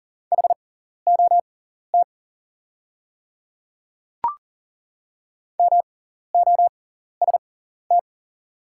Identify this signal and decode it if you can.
Morse code, heard as a keyed tone, about 40 wpm, spaced as for 14 wpm, 700 Hz. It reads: HOT MOST
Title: 6x Character Spacing - Top 100 Words - 40wpm